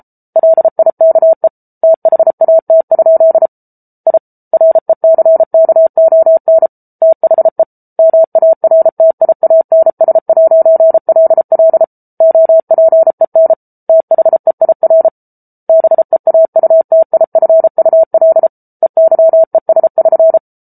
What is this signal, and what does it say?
Morse code, heard as a keyed tone, about 34 wpm, 650 Hz. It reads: LIKE THAT? I RECKON THE MARTIANS'LL OPEN THEIR BEAUTIFUL EYES<SN>